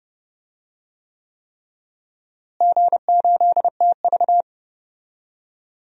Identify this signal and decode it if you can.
Morse code, heard as a keyed tone, about 30 wpm, 700 Hz. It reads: G8TV
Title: DXCC-100 Call Signs - 4 Characters - Repeated 3x - 30wpm